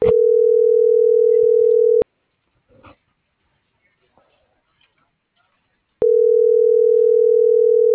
Ringback tone (North America), 440 + 480 Hz.